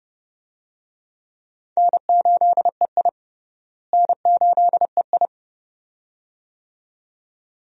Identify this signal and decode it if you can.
Morse code, heard as a keyed tone, about 31 wpm, 700 Hz. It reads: N8EI N8EI
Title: USA Call Signs - 4 Characters - Repeated 2x - 30wpm